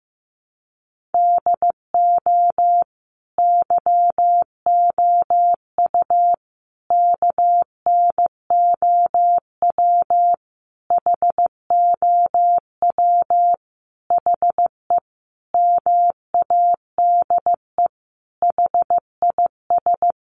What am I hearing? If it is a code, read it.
Morse code, heard as a keyed tone, about 15 wpm, 700 Hz. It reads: DO YOU KNOW HOW HE MADE HIS
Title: Sentences from Top 300 Words 15wpm